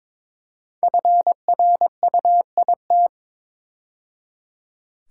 Morse code, heard as a keyed tone, about 22 wpm, 690 Hz. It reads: FRUIT
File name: 5 Letter Words - First 3 Letters Given - 22wpm